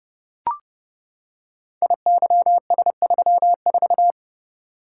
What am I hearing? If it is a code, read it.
Morse code, heard as a keyed tone, about 30 wpm, 700 Hz. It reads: IYS34
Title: Code Groups - 5 Letter:Numbers - 30wpm